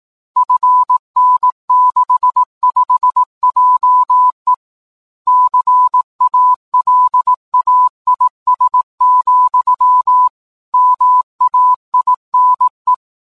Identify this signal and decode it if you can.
Morse code, heard as a keyed tone, about 18 wpm, 1000 Hz. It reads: FN65JE CALAIS, MAINE
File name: 10m-beacon